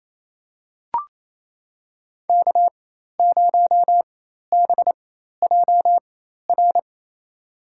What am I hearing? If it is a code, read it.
Morse code, heard as a keyed tone, about 28 wpm, spaced as for 13 wpm, 700 Hz. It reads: K0BJR
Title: USA Call Signs - 5 Characters - 4x Character Spacing - 28wpm